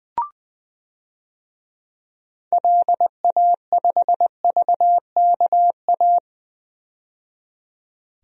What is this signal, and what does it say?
Morse code, heard as a keyed tone, about 20 wpm, 700 Hz. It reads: LA5VKA
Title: DXCC-100 Call Signs - 6 Characters - 20wpm